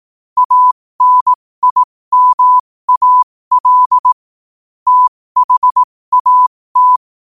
Morse code, heard as a keyed tone, about 18 wpm, spaced as for 15 wpm, 1000 Hz. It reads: ANIMAL THAT
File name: qotdfiles / 15wpm